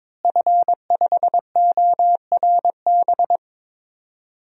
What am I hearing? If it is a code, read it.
Morse code, heard as a keyed tone, about 22 wpm, 700 Hz. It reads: F5ORB
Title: DXCC-100 Call Signs - 5 Characters - Repeated 2x - 22wpm